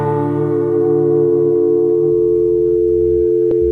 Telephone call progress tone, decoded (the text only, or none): dial tone